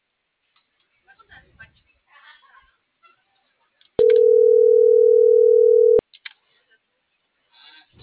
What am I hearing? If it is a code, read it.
Ringback tone (North America), 440 + 480 Hz.